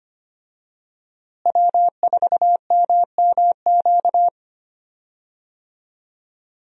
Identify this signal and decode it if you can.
Morse code, heard as a keyed tone, about 25 wpm, 700 Hz. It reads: W4MMQ
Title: USA Call Signs - 5 Characters - Repeated 3x - 25wpm